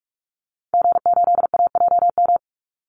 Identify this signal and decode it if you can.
Morse code, heard as a keyed tone, about 44 wpm, 700 Hz. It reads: G8AJM
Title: Call Signs Repeated 2x - 45wpm